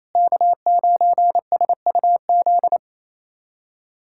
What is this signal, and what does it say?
Morse code, heard as a keyed tone, about 28 wpm, 700 Hz. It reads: K9SUZ
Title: USA Callsign DE Callsign - 5 Characters - 28wpm